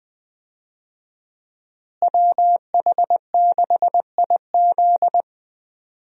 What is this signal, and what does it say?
Morse code, heard as a keyed tone, about 20 wpm, 700 Hz. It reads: WH6IZ
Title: USA Call Signs - 5 Characters - Repeated 2x - 20wpm